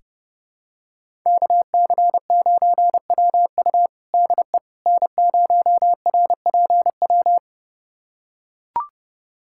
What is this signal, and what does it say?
Morse code, heard as a keyed tone, about 30 wpm, 700 Hz. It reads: KC9WU DE N0RPW
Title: USA Callsign DE Callsign - 5 Characters - 30wpm